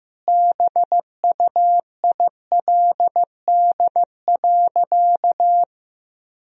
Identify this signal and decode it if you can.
Morse code, heard as a keyed tone, about 15 wpm, 700 Hz. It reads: BUILD.
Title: Sentences from Top 1000 Words - Review Part 1 - 15wpm